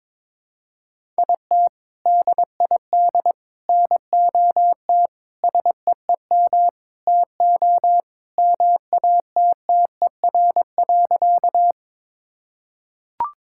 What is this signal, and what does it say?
Morse code, heard as a keyed tone, about 22 wpm, 700 Hz. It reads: IT DID NOT SEEM TO MATTER.
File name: practice / 22wpm / Sentences with 4-Word Phrases - 22wpm